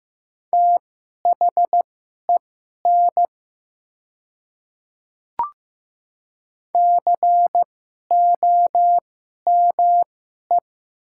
Morse code, heard as a keyed tone, about 15 wpm, spaced as for 11 wpm, 700 Hz. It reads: THEN COME
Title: 2x Character Spacing - Top 100 Words - 15wpm